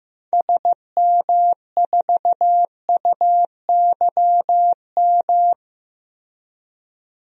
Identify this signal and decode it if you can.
Morse code, heard as a keyed tone, about 16 wpm, 700 Hz. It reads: SM4UYM
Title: DXCC-100 Call Signs - 6 Characters - Repeated 2x - 15wpm